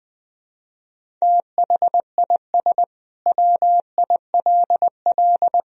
Morse code, heard as a keyed tone, about 21 wpm, 700 Hz. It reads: THIS WILL
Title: Sentences from Top 400 Words and Abbreviations 20wpm